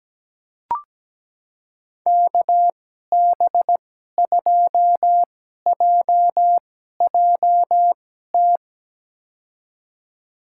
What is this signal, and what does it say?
Morse code, heard as a keyed tone, about 17 wpm, spaced as for 12 wpm, 700 Hz. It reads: KB2JJT
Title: USA Call Signs - 6 Characters - 2x Character Spacing - 17wpm